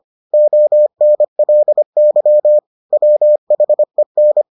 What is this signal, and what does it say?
Morse code, heard as a keyed tone, about 25 wpm, 600 Hz. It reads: ONLY WHEN